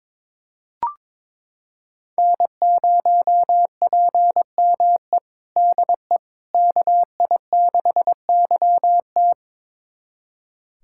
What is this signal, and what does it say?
Morse code, heard as a keyed tone, about 21 wpm, 700 Hz. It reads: N0PME DE KI6YT
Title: USA Callsign DE Callsign - 5 Characters - 22wpm